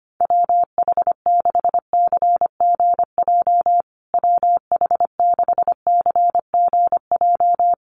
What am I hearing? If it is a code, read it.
Morse code, heard as a keyed tone, about 25 wpm, 700 Hz. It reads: WH6CGJ WH6CGJ
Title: USA Call Signs Repeated 3x 25wpm